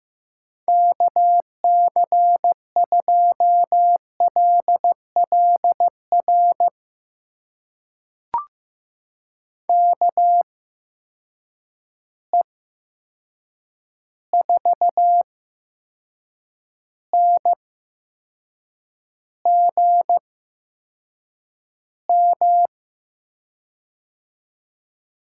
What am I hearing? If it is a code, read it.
Morse code, heard as a keyed tone, about 15 wpm, spaced as for 6 wpm, 700 Hz. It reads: KC2LLR KE4NGM
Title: USA Call Signs - 6 Characters - 8x Character Spacing - 15wpm